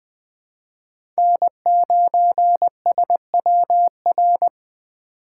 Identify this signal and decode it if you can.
Morse code, heard as a keyed tone, about 20 wpm, 700 Hz. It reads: N9SWR
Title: USA Call Signs - 5 Characters - 4x Character Spacing - 20wpm